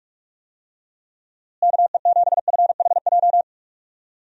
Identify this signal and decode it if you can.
Morse code, heard as a keyed tone, about 46 wpm, 700 Hz. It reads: KE7FHJ
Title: USA Call Signs - 6 Characters - Repeated 2x - 45wpm